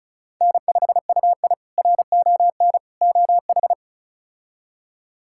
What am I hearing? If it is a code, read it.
Morse code, heard as a keyed tone, about 36 wpm, 700 Hz. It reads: N5UI RON OH
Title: K1USN SST QSO Practice - 35wpm